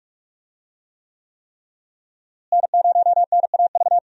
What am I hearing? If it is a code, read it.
Morse code, heard as a keyed tone, about 46 wpm, 700 Hz. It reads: N0NAV
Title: USA Call Signs - 5 Characters - Repeated 3x - 45wpm